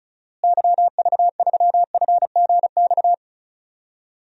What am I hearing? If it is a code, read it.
Morse code, heard as a keyed tone, about 34 wpm, 700 Hz. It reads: YV3FGX
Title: DXCC-100 Call Signs - 6 Characters - Repeated 2x - 35wpm